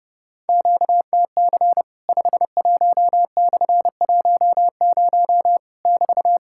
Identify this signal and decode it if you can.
Morse code, heard as a keyed tone, about 32 wpm, 700 Hz. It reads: QTC 51/10 <BT>
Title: Worked All Europe DX Contest - QTC Exchange - 30wpm